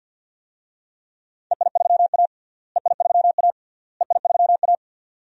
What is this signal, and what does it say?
Morse code, heard as a keyed tone, about 53 wpm, 700 Hz. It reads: EI3A EI3A EI3A
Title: DXCC-100 Call Signs - 4 Characters - Repeated 3x - 50wpm